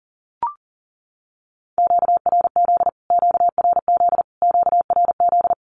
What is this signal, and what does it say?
Morse code, heard as a keyed tone, about 40 wpm, 700 Hz. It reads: QRZ QRZ QRZ
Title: Q-Codes Repeated 3x 40wpm